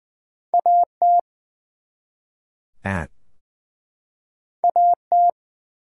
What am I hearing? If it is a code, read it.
Morse code, heard as a keyed tone, about 20 wpm, 700 Hz. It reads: AT AT
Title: Two Letter Words - 20wpm